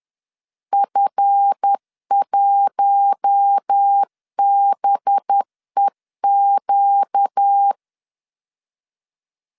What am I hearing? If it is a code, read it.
Morse code, heard as a keyed tone, about 10 wpm, 800 Hz. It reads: F1BEQ
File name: LEZIONI / 10 SIGNS / 10B INT 10wpm 2844